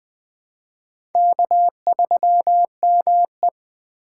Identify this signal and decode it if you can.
Morse code, heard as a keyed tone, about 21 wpm, 700 Hz. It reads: K3ME